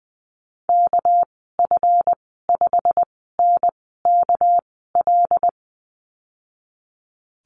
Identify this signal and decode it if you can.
Morse code, heard as a keyed tone, about 20 wpm, spaced as for 15 wpm, 700 Hz. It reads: KF5NKL
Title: 219 Lesson 35g - Number 0 - Call Signs 20wpm